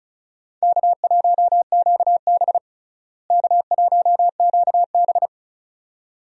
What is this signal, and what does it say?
Morse code, heard as a keyed tone, about 36 wpm, 700 Hz. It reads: K1QB K1QB
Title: USA Call Signs - 4 Characters - Repeated 2x - 35wpm